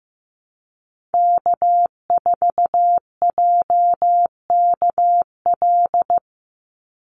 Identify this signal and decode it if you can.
Morse code, heard as a keyed tone, about 15 wpm, 700 Hz. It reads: K4JKL
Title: Call Signs Repeated 2x - 15wpm